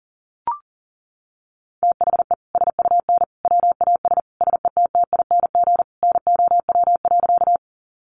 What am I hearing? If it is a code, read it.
Morse code, heard as a keyed tone, about 40 wpm, 700 Hz. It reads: THE SUN WAS SETTING NOW.
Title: Sentences from Top 1300 Words 40wpm